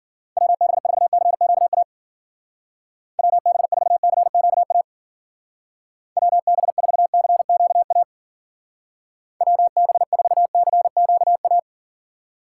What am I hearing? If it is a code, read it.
Morse code, heard as a keyed tone, about 51 wpm, 700 Hz. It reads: WB4CQA WB4CQA WB4CQA WB4CQA